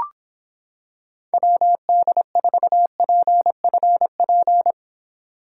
Morse code, heard as a keyed tone, about 26 wpm, 700 Hz. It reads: WD4PFP